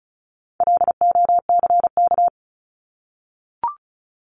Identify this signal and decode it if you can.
Morse code, heard as a keyed tone, about 34 wpm, 700 Hz. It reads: LOCK
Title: Top 1001-1250 Words 35wpm